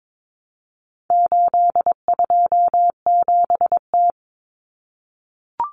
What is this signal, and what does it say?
Morse code, heard as a keyed tone, about 22 wpm, 700 Hz. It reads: 827T
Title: Common Cut Numbers - 4-Digit 22wpm